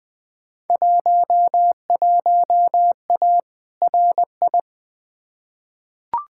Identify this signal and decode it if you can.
Morse code, heard as a keyed tone, about 20 wpm, 700 Hz. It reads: 11A RI
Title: Field Day Exchanges Encore - 20wpm